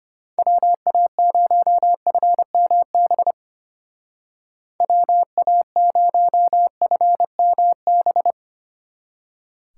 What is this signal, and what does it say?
Morse code, heard as a keyed tone, about 32 wpm, 700 Hz. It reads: WA0FMB WA0FMB